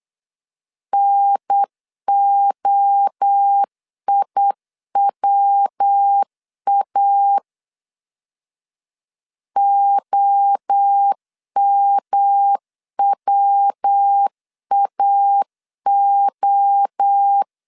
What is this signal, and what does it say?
Morse code, heard as a keyed tone, about 8 wpm, 800 Hz. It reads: NOIWA OMWAO